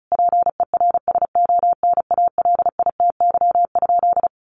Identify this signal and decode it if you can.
Morse code, heard as a keyed tone, about 34 wpm, 700 Hz. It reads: PERSONALITY?
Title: Sentences from Top 1400 Words 35wpm